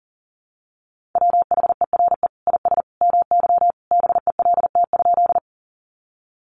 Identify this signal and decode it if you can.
Morse code, heard as a keyed tone, about 37 wpm, 700 Hz. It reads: WHERE IS MY BELT?
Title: Sentences from Top 1400 Words 40wpm